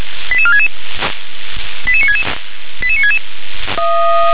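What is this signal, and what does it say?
Five-tone selective call (ZVEI), heard as quick runs of five tones, tones of about 80 ms. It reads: 88466 88966 88966
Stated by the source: Five-tone ZVEI selective call 88965, received on SDR.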